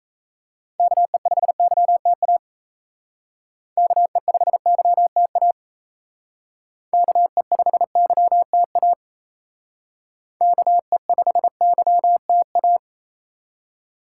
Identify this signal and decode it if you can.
Morse code, heard as a keyed tone, about 35 wpm, 700 Hz. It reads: KE5YTA KE5YTA KE5YTA KE5YTA